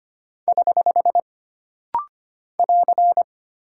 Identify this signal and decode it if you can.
Morse code, heard as a keyed tone, about 25 wpm, 700 Hz. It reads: <HH> <AR>
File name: Single Punctuation-Prosign - Rapid-Fire - 25wpm